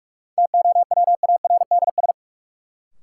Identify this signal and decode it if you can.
Morse code, heard as a keyed tone, about 46 wpm, 700 Hz. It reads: TOWARDS
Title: Sentences from Top 1100 Words - New Word Drill Alt - 45wpm